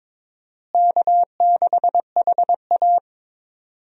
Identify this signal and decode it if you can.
Morse code, heard as a keyed tone, about 22 wpm, 700 Hz. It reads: K6HA
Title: USA Call Signs - 4 Characters - Repeated 3x - 22wpm